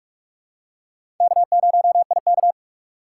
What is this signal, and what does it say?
Morse code, heard as a keyed tone, about 43 wpm, 700 Hz. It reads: K0IK
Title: USA Call Signs - 3x Character Spacing - 45wpm